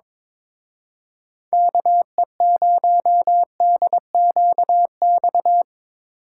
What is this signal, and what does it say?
Morse code, heard as a keyed tone, about 25 wpm, 700 Hz. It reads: KE0DQX
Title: Speed Racing - USA 2x3 Call Signs 22wpm